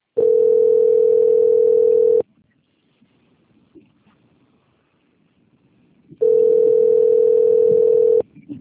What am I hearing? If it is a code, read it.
Ringback tone (North America), 440 + 480 Hz.
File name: out- - - - - .11074